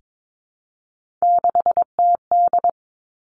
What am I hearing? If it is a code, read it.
Morse code, heard as a keyed tone, about 22 wpm, 700 Hz. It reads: 6TD
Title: Three Challenging Letter-Number Combinations 22wpm